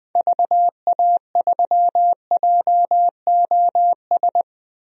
Morse code, heard as a keyed tone, about 20 wpm, 700 Hz. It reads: VA3JOS
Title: DXCC-100 Call Signs - 6 Characters - 20wpm